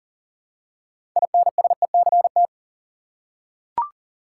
Morse code, heard as a keyed tone, about 40 wpm, 700 Hz. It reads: INSECT
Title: Six Letter Words - 40wpm